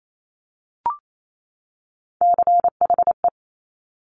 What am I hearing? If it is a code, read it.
Morse code, heard as a keyed tone, about 28 wpm, 700 Hz. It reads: CHE
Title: Top Three Letter Combinations in Words 28wpm